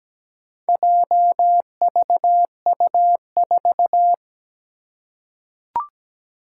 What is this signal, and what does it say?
Morse code, heard as a keyed tone, about 17 wpm, 700 Hz. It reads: JVU4
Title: Four Challenging Letter-Number Combinations 17wpm